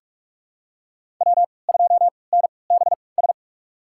Morse code, heard as a keyed tone, about 44 wpm, spaced as for 25 wpm, 700 Hz. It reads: W2NBS
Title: USA Call Signs - 5 Characters - 3x Character Spacing - 45wpm